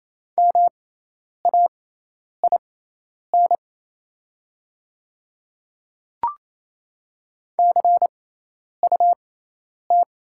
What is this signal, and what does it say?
Morse code, heard as a keyed tone, about 27 wpm, spaced as for 10 wpm, 690 Hz. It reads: MAIN CUT
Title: 6x Character Spacing - Top 300 Words - 28wpm